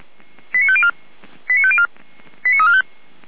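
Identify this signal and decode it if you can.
Five-tone selective call (ZVEI), heard as quick runs of five tones, tones of about 70 ms. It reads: 89504 89504 89345
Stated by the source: ZVEI five-tone selective call received by SDR, logged on the audio server of Feuerwehr Orlamünde fire brigade.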